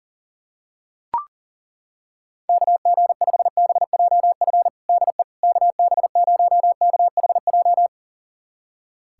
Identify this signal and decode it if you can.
Morse code, heard as a keyed tone, about 37 wpm, 700 Hz. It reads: KG5BJF DE KB0KHJ
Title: USA Callsign DE Callsign - 6 Characters - 40wpm